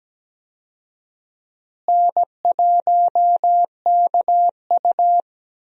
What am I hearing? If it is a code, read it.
Morse code, heard as a keyed tone, about 17 wpm, 700 Hz. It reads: N1KU
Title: USA Call Signs - 4 Characters - Repeated 2x - 17wpm